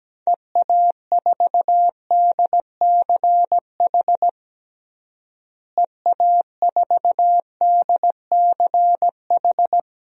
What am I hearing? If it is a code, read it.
Morse code, heard as a keyed tone, about 17 wpm, 700 Hz. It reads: EA4DCH EA4DCH
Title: DXCC-100 Call Signs - 6 Characters - Repeated 2x - 17wpm